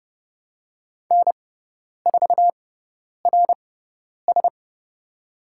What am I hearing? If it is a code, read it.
Morse code, heard as a keyed tone, about 29 wpm, spaced as for 10 wpm, 700 Hz. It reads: N4RS
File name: Lesson 17g - Letter u - Call Signs 30wpm